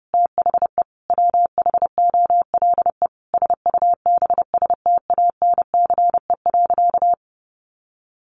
Morse code, heard as a keyed tone, about 30 wpm, 700 Hz. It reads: THE WHOLE SUBSTANCE.
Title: Sentences from Top 1500 Words 30wpm